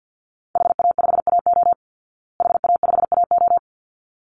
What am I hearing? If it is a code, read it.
Morse code, heard as a keyed tone, about 48 wpm, 700 Hz. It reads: HA5AO HA5AO
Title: Call Signs Repeated 3x - 50wpm